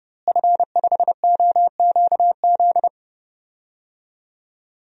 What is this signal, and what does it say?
Morse code, heard as a keyed tone, about 30 wpm, 700 Hz. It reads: F5OQZ
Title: DXCC-100 Call Signs - 5 Characters - Repeated 2x - 30wpm